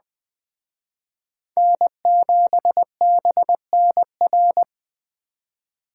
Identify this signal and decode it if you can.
Morse code, heard as a keyed tone, about 21 wpm, 700 Hz. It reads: N7BNR
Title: Speed Racing - USA 1x3 Call Signs 17wpm